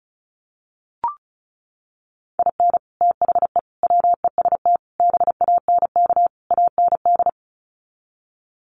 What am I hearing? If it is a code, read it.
Morse code, heard as a keyed tone, about 34 wpm, 700 Hz. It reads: IN THE WEST BANK AND
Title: Sets of 5 Words Encore2 35wpm